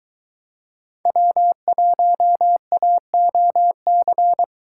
Morse code, heard as a keyed tone, about 21 wpm, 700 Hz. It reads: W1AOC